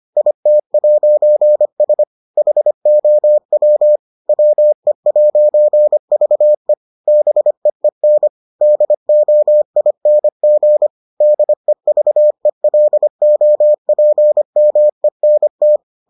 Morse code, heard as a keyed tone, about 25 wpm, 600 Hz. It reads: IT'S HOW WE'VE BEEN DOING DEVELOPMENT